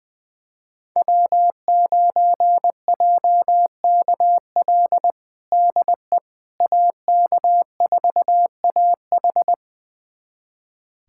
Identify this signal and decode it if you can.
Morse code, heard as a keyed tone, about 21 wpm, 700 Hz. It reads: W9JKL DE AK4AH
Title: USA Callsign DE Callsign 20wpm